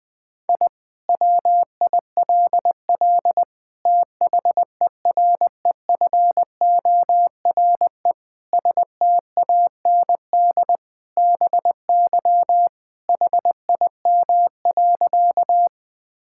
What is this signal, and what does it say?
Morse code, heard as a keyed tone, about 20 wpm, 700 Hz. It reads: I WILL THEREFORE STAND BY HIM.
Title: Sentences from Top 1000 Words - Review Part 5 - 20wpm